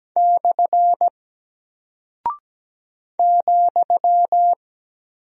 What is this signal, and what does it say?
Morse code, heard as a keyed tone, about 17 wpm, 700 Hz. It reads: / ,
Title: Single Character (Letter:Number:Punctuation:Prosign) Alt 17wpm